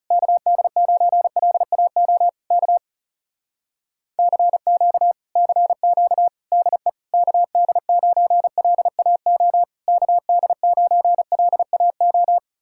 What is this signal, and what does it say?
Morse code, heard as a keyed tone, about 40 wpm, 700 Hz. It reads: KD9LAO K CQ CQ DE KD9LAO KD9LAO